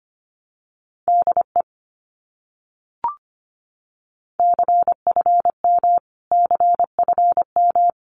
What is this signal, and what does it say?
Morse code, heard as a keyed tone, about 25 wpm, 700 Hz. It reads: DE CFM CFM
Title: Abbreviations Repeated 3x - 25wpm